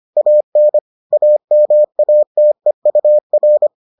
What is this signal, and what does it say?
Morse code, heard as a keyed tone, about 25 wpm, 600 Hz. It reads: AN AMATEUR